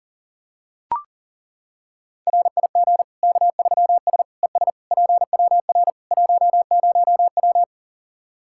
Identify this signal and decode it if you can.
Morse code, heard as a keyed tone, about 40 wpm, 700 Hz. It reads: RIG K3S ES PWR 10W
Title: Practice QSO - Exchange 2 - Info - 40wpm